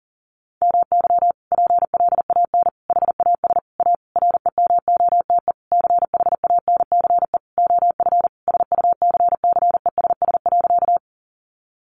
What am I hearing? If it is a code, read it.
Morse code, heard as a keyed tone, about 40 wpm, 700 Hz. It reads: MY PLAN HAS A REMOTE CHANCE OF SUCCESS.
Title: Sentences from Top 1300 Words 40wpm